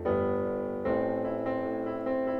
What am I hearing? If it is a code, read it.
There is no code.